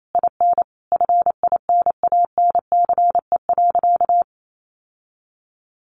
Morse code, heard as a keyed tone, about 28 wpm, 700 Hz. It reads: IN FINANCE.
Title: Sentences from Top 1500 Words 28wpm